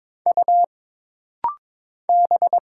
Morse code, heard as a keyed tone, about 22 wpm, 700 Hz. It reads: U B
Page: This rapid-fire timing is geared to push your ICR (Instant Character Recognition) skills.